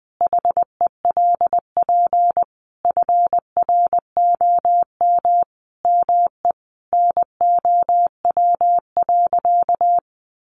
Morse code, heard as a keyed tone, about 20 wpm, 700 Hz. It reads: HELP FROM ME NOW.
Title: Sentences from Top 1300 Words 20wpm